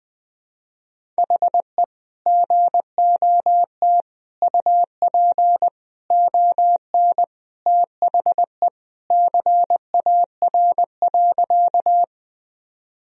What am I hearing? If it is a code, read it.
Morse code, heard as a keyed tone, about 19 wpm, 700 Hz. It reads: HE GOT UP ON THE CAR.
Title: Sentences from Top 1000 Words - Review Part 1 - 20wpm